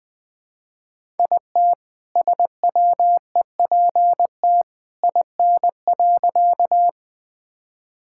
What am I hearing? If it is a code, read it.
Morse code, heard as a keyed tone, about 21 wpm, 700 Hz. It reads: IT SWEPT IN.
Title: All-Age Appropriate Jokes - 20wpm